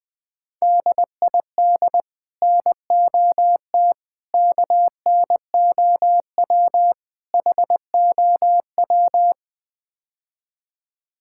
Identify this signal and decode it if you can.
Morse code, heard as a keyed tone, about 21 wpm, 700 Hz. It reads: DID NOT KNOW HOW
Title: Build A Phrase - 5-Word Phrases - 20wpm